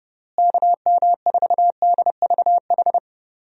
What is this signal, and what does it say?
Morse code, heard as a keyed tone, about 31 wpm, 700 Hz. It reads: KM4DVH
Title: USA Call Signs - 6 Characters - Repeated 3x - 30wpm